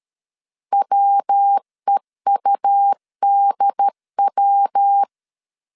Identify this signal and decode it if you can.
Morse code, heard as a keyed tone, about 13 wpm, 800 Hz. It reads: WEUDW
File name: CW / DET / 1 3 1500C S C 12wpm 2750